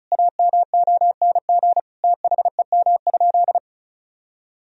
Morse code, heard as a keyed tone, about 34 wpm, 700 Hz. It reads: AMONG THEM?
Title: Sentences from Top 1000 Words - Review Part 2 - 35wpm